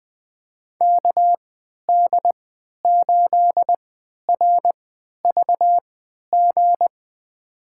Morse code, heard as a keyed tone, about 21 wpm, spaced as for 11 wpm, 700 Hz. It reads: KD8RVG
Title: USA Call Signs - 6 Characters - 3x Character Spacing - 20wpm